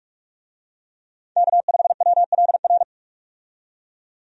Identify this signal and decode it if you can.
Morse code, heard as a keyed tone, about 46 wpm, 700 Hz. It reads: K5WLR
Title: USA Call Signs - 5 Characters - 45wpm